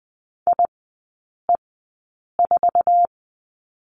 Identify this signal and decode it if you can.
Morse code, heard as a keyed tone, about 20 wpm, spaced as for 8 wpm, 700 Hz. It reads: IE4
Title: 047 Lesson 9e - Number 4 - Three Characters 20wpm